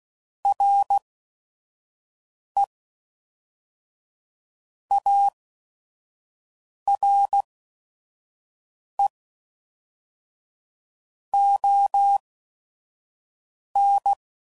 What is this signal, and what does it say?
Morse code, heard as a keyed tone, about 16 wpm, spaced as for 5 wpm, 800 Hz. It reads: RE ARE ON
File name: CW Code Groups @5WPM / Technical Subjects / AND Gate